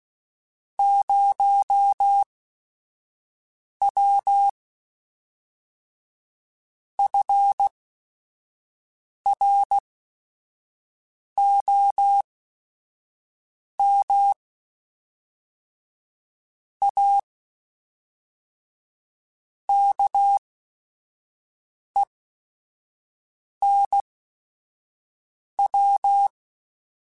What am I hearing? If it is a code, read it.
Morse code, heard as a keyed tone, about 16 wpm, spaced as for 5 wpm, 800 Hz. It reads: 0W FROM A KENW